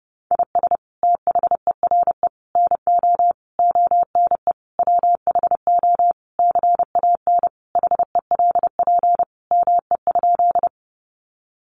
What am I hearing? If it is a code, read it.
Morse code, heard as a keyed tone, about 30 wpm, 700 Hz. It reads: IS THERE NO ONE WHO CAN HELP ME?
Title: Sentences from Top 200 Words 30wpm